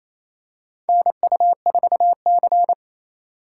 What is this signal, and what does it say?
Morse code, heard as a keyed tone, about 28 wpm, 700 Hz. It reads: NU4C
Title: USA Call Signs - 4 Characters - 4x Character Spacing - 28wpm